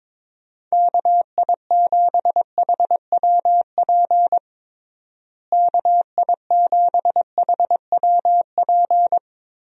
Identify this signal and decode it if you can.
Morse code, heard as a keyed tone, about 22 wpm, 700 Hz. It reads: KI7HWP KI7HWP
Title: USA Call Signs - 6 Characters - Repeated 3x - 22wpm